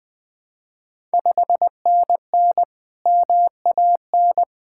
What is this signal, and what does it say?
Morse code, heard as a keyed tone, about 20 wpm, 700 Hz. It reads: 5NN MAN
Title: Ontario QSO Party Exchange 20wpm